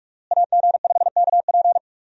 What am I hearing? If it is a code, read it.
Morse code, heard as a keyed tone, about 46 wpm, 700 Hz. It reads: AG5KP
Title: USA Call Signs - 5 Characters - Repeated 2x - 45wpm